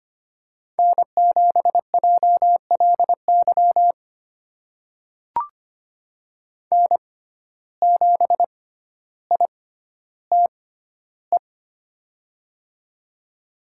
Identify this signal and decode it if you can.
Morse code, heard as a keyed tone, about 25 wpm, spaced as for 13 wpm, 700 Hz. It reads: N7JLY N7ITE
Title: USA Call Signs - 5 Characters - 6x Character Spacing - 25wpm